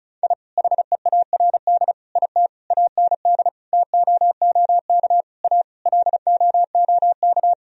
Morse code, heard as a keyed tone, about 34 wpm, 700 Hz. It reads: I HEARD IT AND TOOK A LOOK
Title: Sentences from Top 1000 Words - Review Part 1 - 35wpm